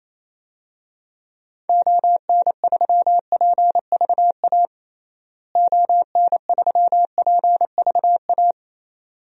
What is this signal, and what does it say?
Morse code, heard as a keyed tone, about 28 wpm, 700 Hz. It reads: ON3PVA ON3PVA
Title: DXCC-100 Call Signs - 6 Characters - Repeated 2x - 28wpm